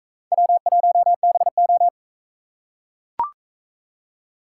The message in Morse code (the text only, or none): W1BO